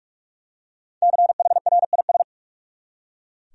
Morse code, heard as a keyed tone, about 41 wpm, 700 Hz. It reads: CHRIS
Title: USA Common Names - 45wpm